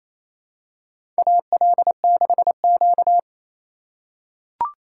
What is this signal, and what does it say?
Morse code, heard as a keyed tone, about 27 wpm, 700 Hz. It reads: AL6Q
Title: USA Call Signs - 4 Characters - 2x Character Spacing - 28wpm